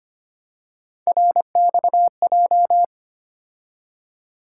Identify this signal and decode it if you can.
Morse code, heard as a keyed tone, about 25 wpm, 700 Hz. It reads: RXJ
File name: Three Letter-Number Combinations - 25wpm